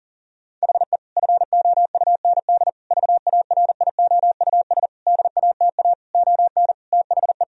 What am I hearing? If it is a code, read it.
Morse code, heard as a keyed tone, about 40 wpm, 700 Hz. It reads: HE FOUND VARIOUS DATA ON THE
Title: Sentences from Top 1000 Words - Review Part 4 - 40wpm